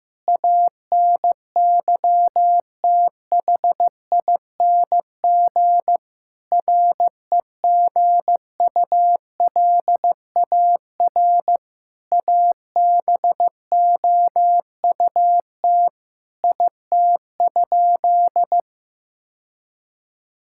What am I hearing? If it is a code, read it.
Morse code, heard as a keyed tone, about 15 wpm, 700 Hz. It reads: ANYTHING REGULAR ABOUT IT?
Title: Sentences from Top 1000 Words - Review Part 6 - 15wpm